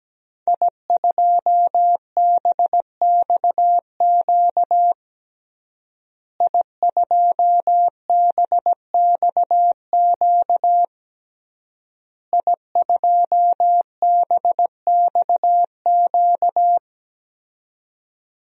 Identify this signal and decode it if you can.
Morse code, heard as a keyed tone, about 17 wpm, 700 Hz. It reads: I2BXQ I2BXQ I2BXQ
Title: DXCC-100 Call Signs - 5 Characters - Repeated 3x - 17wpm